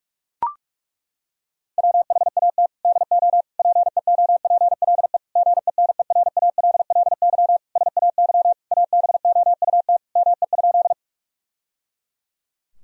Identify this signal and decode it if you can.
Morse code, heard as a keyed tone, about 42 wpm, 700 Hz. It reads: WHAT DO PEOPLE GENERALLY SAY ABOUT ME?
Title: Sentences from Top 1000 Words - Review Part 4 - 45wpm